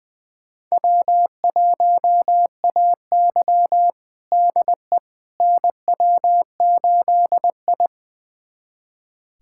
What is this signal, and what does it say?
Morse code, heard as a keyed tone, about 20 wpm, 700 Hz. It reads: W1AY DE NW8I
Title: USA Callsign DE Callsign - 4 Characters - 20wpm